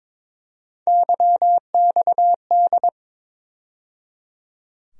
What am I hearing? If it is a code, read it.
Morse code, heard as a keyed tone, about 22 wpm, 700 Hz. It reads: YXD
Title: USA Call Sign Postfixes - 3 Letters 22wpm